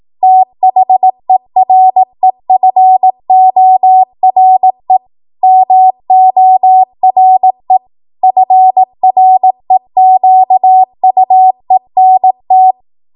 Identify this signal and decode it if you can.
Morse code, heard as a keyed tone, about 18 wpm, 750 Hz. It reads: THEREFORE MORE FREQUENT
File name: Archive / 18 WPM / 18WPM